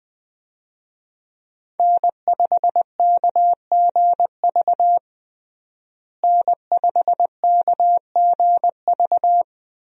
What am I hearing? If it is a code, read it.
Morse code, heard as a keyed tone, about 20 wpm, 700 Hz. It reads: N5KGV N5KGV
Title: USA Call Signs - 5 Characters - Repeated 2x - 20wpm